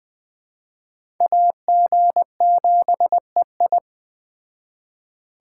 Morse code, heard as a keyed tone, about 21 wpm, 700 Hz. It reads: AG7EI